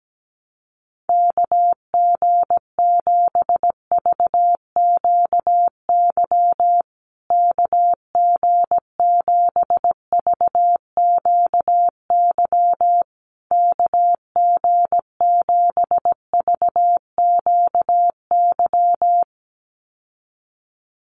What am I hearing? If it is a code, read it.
Morse code, heard as a keyed tone, about 17 wpm, 700 Hz. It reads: KG7VQY KG7VQY KG7VQY
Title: USA Call Signs Repeated 3x 17wpm